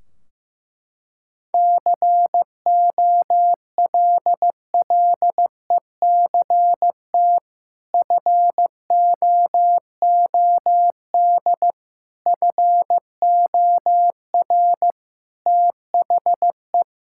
Morse code, heard as a keyed tone, about 15 wpm, 700 Hz. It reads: COLLECT FOOD FOR THE
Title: Sentences from Top 1600 Words 15wpm